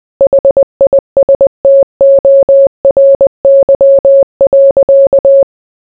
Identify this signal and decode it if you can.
Morse code, heard as a keyed tone, about 20 wpm, 550 Hz. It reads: HISTORY.